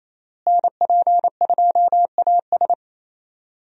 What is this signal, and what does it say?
Morse code, heard as a keyed tone, about 28 wpm, 700 Hz. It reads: NP2AS
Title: USA Call Signs - 5 Characters - Repeated 3x - 28wpm